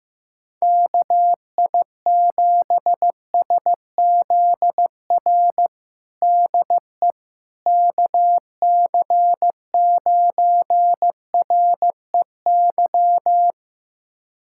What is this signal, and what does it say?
Morse code, heard as a keyed tone, about 15 wpm, 700 Hz. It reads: KI7SZR DE KC9REY
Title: USA Callsign DE Callsign - 6 Characters - 15wpm